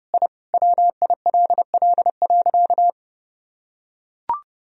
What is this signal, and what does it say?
Morse code, heard as a keyed tone, about 29 wpm, 700 Hz. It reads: I WILL.
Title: Sentences from Top 1000 Words - Review Part 1 - 30wpm